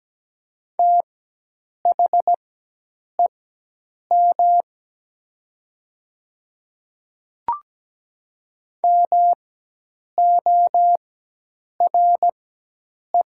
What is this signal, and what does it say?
Morse code, heard as a keyed tone, about 17 wpm, spaced as for 8 wpm, 700 Hz. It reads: THEM MORE
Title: 4x Character Spacing - Top 100 Words - 17wpm